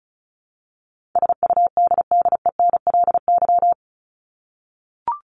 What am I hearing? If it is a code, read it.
Morse code, heard as a keyed tone, about 34 wpm, 700 Hz. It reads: SUDDENLY
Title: Top 1501-1750 Words 35wpm